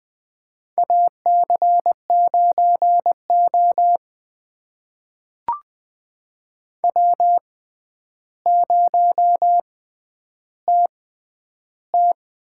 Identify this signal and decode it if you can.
Morse code, heard as a keyed tone, about 21 wpm, spaced as for 10 wpm, 700 Hz. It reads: AC9O W0TT
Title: USA Call Signs - 4 Characters - 6x Character Spacing - 20wpm